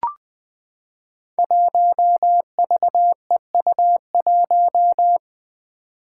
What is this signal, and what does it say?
Morse code, heard as a keyed tone, about 21 wpm, 700 Hz. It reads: 1VEU1
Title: Five Challenging Letter-Number Combinations 20wpm